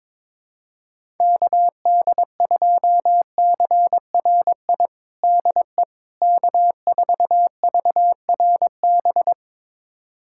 Morse code, heard as a keyed tone, about 22 wpm, 700 Hz. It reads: KD2CRI DE K4VRB
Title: USA Callsign DE Callsign 22wpm